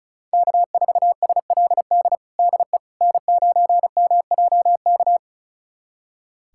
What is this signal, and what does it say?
Morse code, heard as a keyed tone, about 36 wpm, 700 Hz. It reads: K4SLD DE N9MJK
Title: USA Callsign DE Callsign - 5 Characters - 35wpm